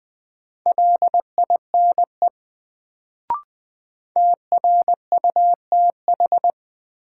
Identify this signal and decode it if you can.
Morse code, heard as a keyed tone, about 20 wpm, 700 Hz. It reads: LINE TRUTH
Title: Sets of 5 Words - New Word Drill - 20wpm